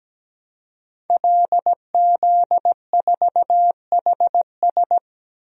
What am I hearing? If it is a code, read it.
Morse code, heard as a keyed tone, about 17 wpm, 700 Hz. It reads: LZ4HS
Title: DXCC-100 Call Signs - 5 Characters - Repeated 3x - 17wpm